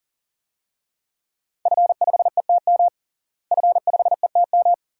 Morse code, heard as a keyed tone, about 40 wpm, 700 Hz. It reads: F5ETM F5ETM
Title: DXCC-100 Call Signs - 5 Characters - Repeated 2x - 40wpm